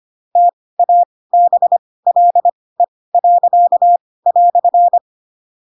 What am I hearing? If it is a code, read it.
Morse code, heard as a keyed tone, about 25 wpm, spaced as for 18 wpm, 700 Hz. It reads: TABLE."
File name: Pinocchio / Pinocchio ch1 12 25 ik1hge